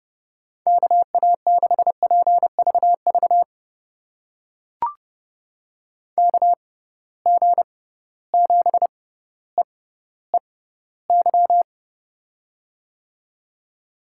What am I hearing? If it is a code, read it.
Morse code, heard as a keyed tone, about 31 wpm, spaced as for 15 wpm, 700 Hz. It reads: KA6PVV KG7EEY
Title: USA Call Signs - 6 Characters - 6x Character Spacing - 30wpm